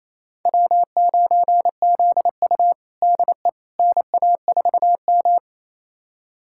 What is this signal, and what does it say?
Morse code, heard as a keyed tone, about 28 wpm, 700 Hz. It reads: W9ZU DE NA4M
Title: USA Callsign DE Callsign - 4 Characters - 28wpm